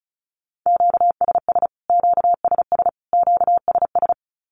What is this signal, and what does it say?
Morse code, heard as a keyed tone, about 34 wpm, 700 Hz. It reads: QSS QSS QSS
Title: Q-Codes Repeated 3x 35wpm